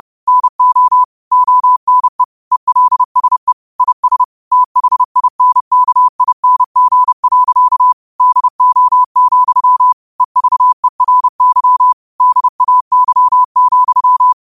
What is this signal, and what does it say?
Morse code, heard as a keyed tone, about 30 wpm, 1000 Hz. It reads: NO ONE ELSE IS THINKING. DO, EVERY DAY,
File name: qotdfiles / 30wpm